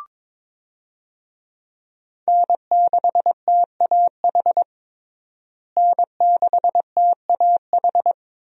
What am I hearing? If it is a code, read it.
Morse code, heard as a keyed tone, about 22 wpm, 700 Hz. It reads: N6TAH N6TAH
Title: USA Call Signs - 5 Characters - Repeated 2x - 22wpm